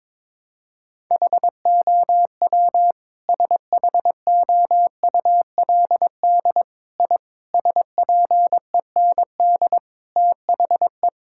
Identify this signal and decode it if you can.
Morse code, heard as a keyed tone, about 21 wpm, 700 Hz. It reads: HOW SHOULD I SPEND THE
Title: Sentences from Top 1000 Words - Review Part 5 - 22wpm